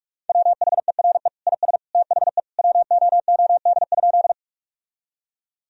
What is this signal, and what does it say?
Morse code, heard as a keyed tone, about 46 wpm, 700 Hz. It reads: WHERE IS THE WOOD?
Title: Sentences from Top 1000 Words - Review Part 2 - 45wpm